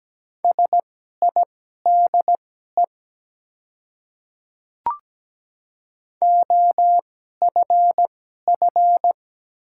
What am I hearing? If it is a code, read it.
Morse code, heard as a keyed tone, about 17 wpm, spaced as for 12 wpm, 700 Hz. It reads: SIDE OFF
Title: 2x Character Spacing - Top 200 Words - 17wpm